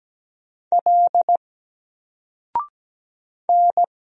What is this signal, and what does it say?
Morse code, heard as a keyed tone, about 17 wpm, 700 Hz. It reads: L N
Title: Single Letter-Number Alt 17wpm